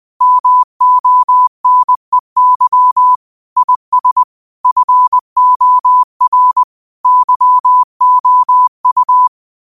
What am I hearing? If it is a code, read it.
Morse code, heard as a keyed tone, about 20 wpm, 1000 Hz. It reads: MONEY IS FOR YOU
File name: qotdfiles / 20wpm